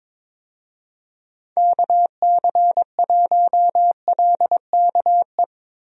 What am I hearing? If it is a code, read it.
Morse code, heard as a keyed tone, about 21 wpm, 700 Hz. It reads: KC1LKE
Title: USA Call Signs - 6 Characters - Repeated 3x - 22wpm